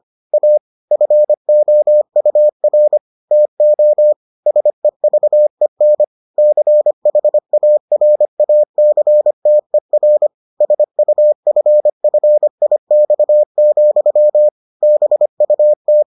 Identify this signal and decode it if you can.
Morse code, heard as a keyed tone, about 25 wpm, 600 Hz. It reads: A FOUR TO SEVEN CHARACTER SUFFIX, BUT